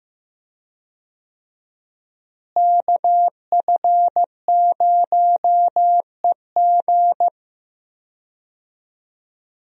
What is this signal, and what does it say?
Morse code, heard as a keyed tone, about 15 wpm, 700 Hz. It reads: KF0EG
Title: USA Call Signs - 5 Characters - Repeated 2x - 15wpm